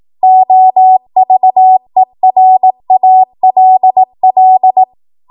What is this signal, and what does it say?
Morse code, heard as a keyed tone, about 18 wpm, 750 Hz. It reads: OVERALL
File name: Archive / 18 WPM / 18WPM